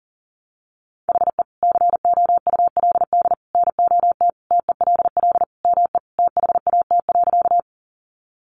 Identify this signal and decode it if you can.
Morse code, heard as a keyed tone, about 40 wpm, 700 Hz. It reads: HE COULD NOT TELL ME THAT.
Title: Sentences from Top 600 Words 40wpm